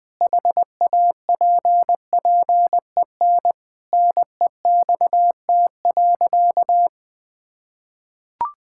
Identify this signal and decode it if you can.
Morse code, heard as a keyed tone, about 19 wpm, 700 Hz. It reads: HAPPEN NEXT.
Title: Sentences with 7-Word Phrases - 20wpm